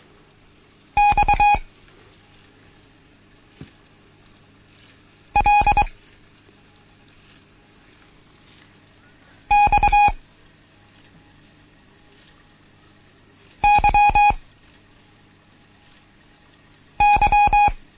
Morse code, heard as a keyed tone, about 23 wpm, 830 Hz. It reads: X L X Y Y
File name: Morsen / Kurs A 16